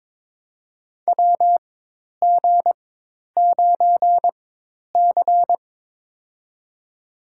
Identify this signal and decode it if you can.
Morse code, heard as a keyed tone, about 23 wpm, spaced as for 10 wpm, 700 Hz. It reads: WG9C
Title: USA Call Signs - 4 Characters - 4x Character Spacing - 22wpm